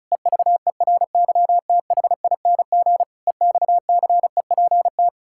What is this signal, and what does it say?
Morse code, heard as a keyed tone, about 34 wpm, 700 Hz. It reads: EVERYTHING EXCEPT
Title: Sentences from Top 1600 Words 35wpm